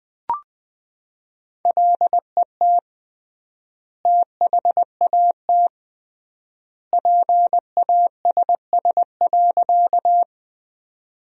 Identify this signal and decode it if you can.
Morse code, heard as a keyed tone, about 20 wpm, 700 Hz. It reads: LET THAT PASS.
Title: 3x Word-Spacing - Sentences from Top 400 Words - 20wpm